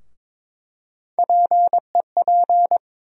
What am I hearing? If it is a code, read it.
Morse code, heard as a keyed tone, about 21 wpm, 700 Hz. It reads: PEP
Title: USA Call Sign Postfixes - 3 Letters 22wpm